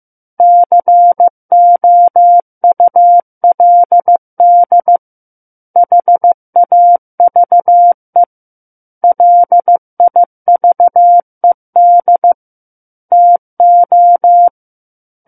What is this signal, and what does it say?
Morse code, heard as a keyed tone, about 15 wpm, 700 Hz. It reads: COULD HAVE LIVED TO
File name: Book One / 06-Chapter 6 - 15 WPM